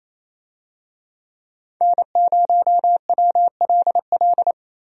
Morse code, heard as a keyed tone, about 28 wpm, 700 Hz. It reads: N0WLL
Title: USA Call Signs - 5 Characters - 28wpm